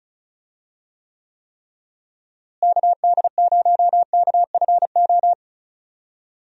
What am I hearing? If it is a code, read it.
Morse code, heard as a keyed tone, about 34 wpm, 700 Hz. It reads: KD0KFO
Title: USA Call Signs - 6 Characters - Repeated 3x - 35wpm